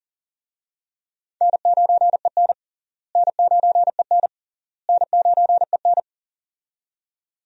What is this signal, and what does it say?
Morse code, heard as a keyed tone, about 40 wpm, 700 Hz. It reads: N9EN N9EN N9EN
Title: USA Call Signs - 4 Characters - Repeated 3x - 40wpm